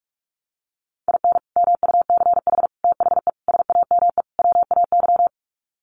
Morse code, heard as a keyed tone, about 44 wpm, 700 Hz. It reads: IN MUCH THE SAME WAY
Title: Sets of 5 Words Encore 45wpm